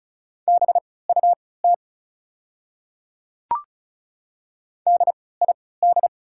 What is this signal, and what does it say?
Morse code, heard as a keyed tone, about 36 wpm, spaced as for 20 wpm, 690 Hz. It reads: BUT DID
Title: 3x Character Spacing - Top 100 Words - 35wpm